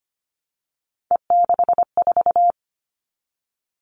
Morse code, heard as a keyed tone, about 25 wpm, 700 Hz. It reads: E64
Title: All Cut Numbers - 3-Digit 25wpm